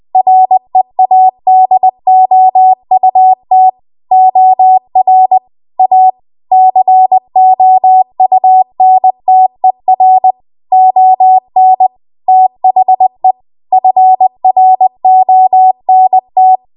Morse code, heard as a keyed tone, about 20 wpm, 750 Hz. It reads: READOUT OR A COUNTER ON THE FRONT
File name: Archive / 20 WPM / 20WPM